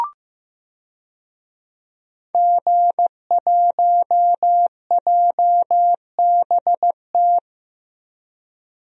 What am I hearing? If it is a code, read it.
Morse code, heard as a keyed tone, about 15 wpm, 700 Hz. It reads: G1JBT
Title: DXCC-100 Call Signs - 5 Characters - 15wpm